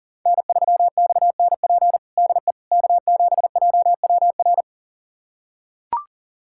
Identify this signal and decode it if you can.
Morse code, heard as a keyed tone, about 37 wpm, 700 Hz. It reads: N3XNP DE K7JWR
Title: USA Callsign DE Callsign - 5 Characters - 40wpm